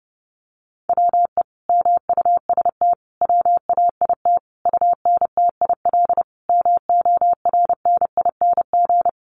Morse code, heard as a keyed tone, about 30 wpm, 700 Hz. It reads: WE MUST WAIT UNTIL MORNING
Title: Sentences from Top 1300 Words 30wpm